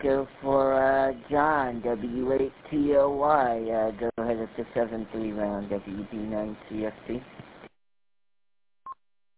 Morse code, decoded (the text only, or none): none